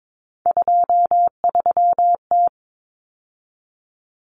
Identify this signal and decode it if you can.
Morse code, heard as a keyed tone, about 22 wpm, 700 Hz. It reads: 23T
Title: Common Cut Numbers - 3-Digit 22wpm